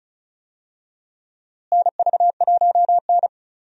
Morse code, heard as a keyed tone, about 36 wpm, 700 Hz. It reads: NV1N
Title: USA Call Signs - 4 Characters - Repeated 3x - 35wpm